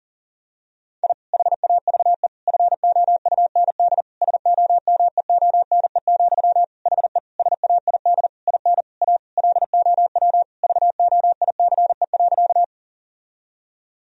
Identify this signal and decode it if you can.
Morse code, heard as a keyed tone, about 37 wpm, 700 Hz. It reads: I HAVE FOUND SOMEONE, HE SAID IN A LOW VOICE.
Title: Sentences from Top 1000 Words - Review Part 1 - 40wpm